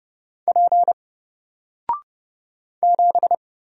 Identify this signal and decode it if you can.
Morse code, heard as a keyed tone, about 29 wpm, 700 Hz. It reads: P 7
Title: Single Letter-Number Alt 30wpm